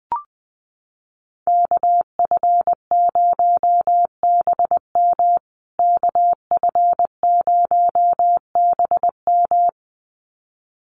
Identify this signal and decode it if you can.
Morse code, heard as a keyed tone, about 20 wpm, 700 Hz. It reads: KF0BM KF0BM
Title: USA Call Signs Repeated 2x 20wpm